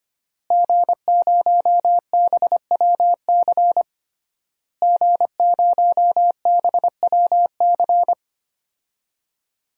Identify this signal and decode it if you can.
Morse code, heard as a keyed tone, about 25 wpm, 700 Hz. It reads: G0BWC G0BWC
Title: DXCC-100 Call Signs - 5 Characters - Repeated 3x - 25wpm